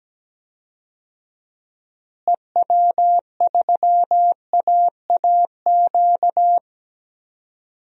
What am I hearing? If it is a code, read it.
Morse code, heard as a keyed tone, about 17 wpm, 700 Hz. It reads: EW3AAQ
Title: DXCC-100 Call Signs - 6 Characters - Repeated 2x - 17wpm